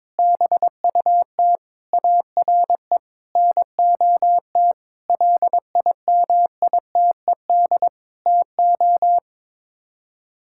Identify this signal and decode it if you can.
Morse code, heard as a keyed tone, about 22 wpm, 700 Hz. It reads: BUT ARE NOT LIMITED TO
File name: practice / 22wpm / Sets of 6 Words - 22wpm